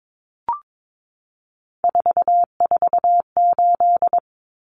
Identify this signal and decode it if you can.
Morse code, heard as a keyed tone, about 22 wpm, 700 Hz. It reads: 448
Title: Numbers - 3-Digit 22wpm